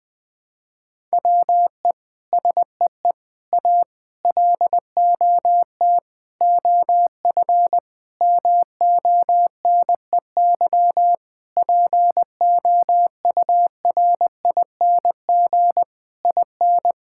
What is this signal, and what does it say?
Morse code, heard as a keyed tone, about 20 wpm, 700 Hz. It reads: WE SEE A LOT OF MONEY POURING IN
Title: Sentences with 9-Word Phrases - 20wpm